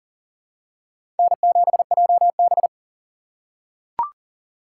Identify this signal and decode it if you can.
Morse code, heard as a keyed tone, about 40 wpm, 700 Hz. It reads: N7JB
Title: USA Call Signs - 4 Characters - 4x Character Spacing - 40wpm